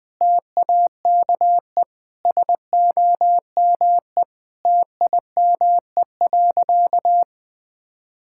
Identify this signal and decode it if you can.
Morse code, heard as a keyed tone, about 20 wpm, 700 Hz. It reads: TAKE SOME TIME.
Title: Sentences with 3-Word Phrases - 20wpm